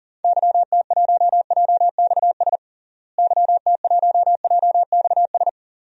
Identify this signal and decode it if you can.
Morse code, heard as a keyed tone, about 40 wpm, 700 Hz. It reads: YT1JXS YT1JXS
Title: DXCC-100 Call Signs - 6 Characters - Repeated 2x - 40wpm